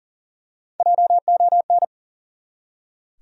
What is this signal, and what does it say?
Morse code, heard as a keyed tone, about 40 wpm, 700 Hz. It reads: JON
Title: USA Common Names - Max 3 Letters - 40wpm